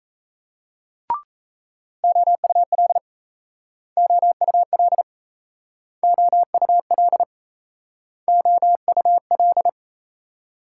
Morse code, heard as a keyed tone, about 36 wpm, 700 Hz. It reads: OUL OUL OUL OUL